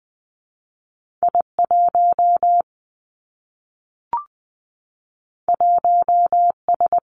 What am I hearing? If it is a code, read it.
Morse code, heard as a keyed tone, about 19 wpm, 700 Hz. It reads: I 1 1 S
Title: Two Challenging Letter-Number Combinations 20wpm